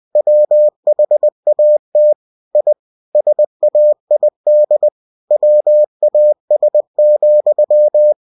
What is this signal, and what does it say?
Morse code, heard as a keyed tone, about 20 wpm, 600 Hz. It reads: WHAT I SAID WAS,
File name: cw-books / winnie-the-pooh / winnie-20wpm-page-0030